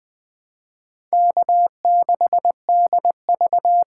Morse code, heard as a keyed tone, about 20 wpm, 700 Hz. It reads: K6DV
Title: USA Call Signs - 6x Character Spacing - 20wpm